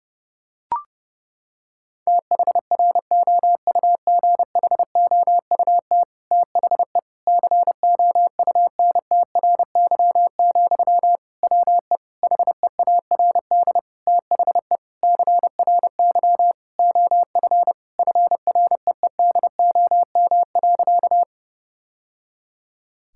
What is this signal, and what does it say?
Morse code, heard as a keyed tone, about 30 wpm, 700 Hz. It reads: THROUGHOUT THE COUNTRY, WE HEARD THE CRY OF FREEDOM.
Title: Sentences from Top 1000 Words - Review Part 5 - 30wpm